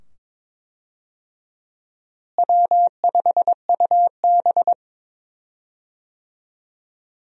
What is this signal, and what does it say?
Morse code, heard as a keyed tone, about 22 wpm, 700 Hz. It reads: W5UB
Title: USA Call Signs - 4 Characters - Repeated 2x - 22wpm